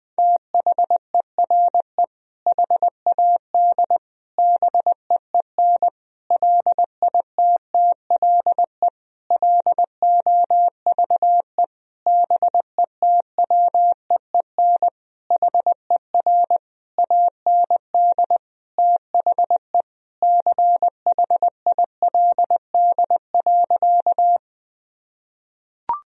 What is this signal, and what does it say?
Morse code, heard as a keyed tone, about 20 wpm, 700 Hz. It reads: THERE HAD BEEN LITTLE LOVE BETWEEN HER AND THE CHILD.
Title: Sentences from Top 1000 Words - Review Part 1 - 20wpm